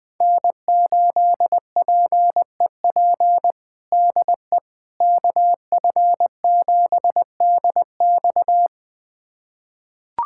Morse code, heard as a keyed tone, about 20 wpm, 700 Hz. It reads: N8PEP DE KF7DX
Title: USA Callsign DE Callsign - 5 Characters - 20wpm